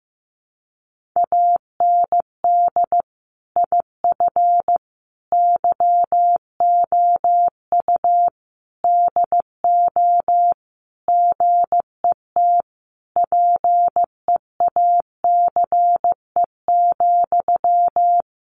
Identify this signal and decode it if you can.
Morse code, heard as a keyed tone, about 15 wpm, 700 Hz. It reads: AND IF YOU DO GET PEACE,
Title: Sentences from Top 1500 Words 15wpm